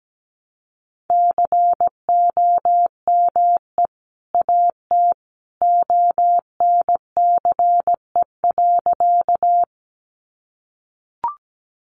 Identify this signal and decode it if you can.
Morse code, heard as a keyed tone, about 17 wpm, 700 Hz. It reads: COME AT ONCE.
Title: Sentences from Top 300 Words 17wpm